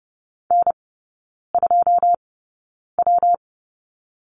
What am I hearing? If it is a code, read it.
Morse code, heard as a keyed tone, about 30 wpm, spaced as for 9 wpm, 700 Hz. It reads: N2W
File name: Lesson 19e - Letter w - Three Characters 30wpm